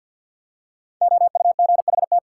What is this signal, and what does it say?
Morse code, heard as a keyed tone, about 48 wpm, 700 Hz. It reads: OUGHT
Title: Five Letter Words - 50wpm